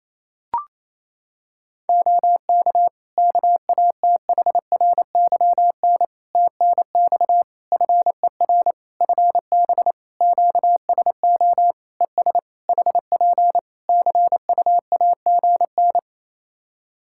Morse code, heard as a keyed tone, about 28 wpm, 700 Hz. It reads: OK KATHRYN TNX FER FB QSO ES HP CUAGN
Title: K1USN SST QSO Practice 28wpm